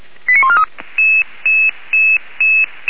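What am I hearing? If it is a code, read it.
Five-tone selective call (ZVEI), heard as quick runs of five tones, tones of about 70 ms. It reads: 89143
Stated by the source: ZVEI five-tone fire brigade alert received by SDR, Feuerwehr Orlamünde.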